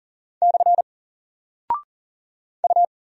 Morse code, heard as a keyed tone, about 40 wpm, 700 Hz. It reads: / U
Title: Single Character (Letter:Number:Punctuation:Prosign) Alt 40wpm